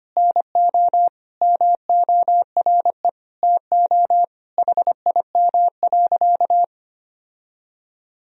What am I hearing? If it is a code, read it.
Morse code, heard as a keyed tone, about 25 wpm, 700 Hz. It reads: NO MORE TO HIM.
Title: Sentences from Top 1000 Words - Review Part 1 - 25wpm